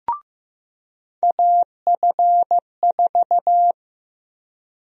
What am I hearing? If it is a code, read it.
Morse code, heard as a keyed tone, about 15 wpm, 700 Hz. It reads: AF4
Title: USA Call Sign Prefixes - 15wpm